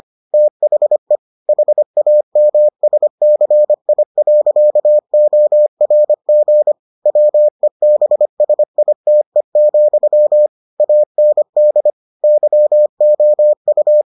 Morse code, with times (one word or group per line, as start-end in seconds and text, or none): 0.3-1.2 THE
1.5-6.7 HAMSCI.ORG
7.1-10.5 WEBSITE,
10.8-11.9 AND
12.2-14.0 YOU